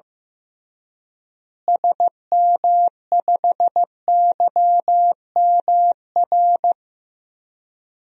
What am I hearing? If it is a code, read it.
Morse code, heard as a keyed tone, about 15 wpm, 700 Hz. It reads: SM5YMR
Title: DXCC-100 Call Signs - 6 Characters - Repeated 3x - 15wpm